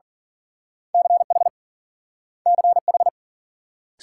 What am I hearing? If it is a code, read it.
Morse code, heard as a keyed tone, about 47 wpm, 700 Hz. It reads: CH CH
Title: Speed-Racing Multi-letter Phonemes - 40wpm